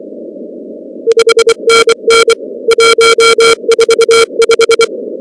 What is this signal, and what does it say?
Morse code, heard as a keyed tone, about 24 wpm, 460 Hz. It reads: 5NN 145
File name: contests / 2019-cw-wpx-kt5j-k5pi- -2330 0-19